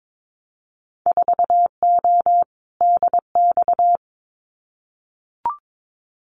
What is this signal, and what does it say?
Morse code, heard as a keyed tone, about 22 wpm, 700 Hz. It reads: 4O DX
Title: Winter Field Day Exchanges Encore 22wpm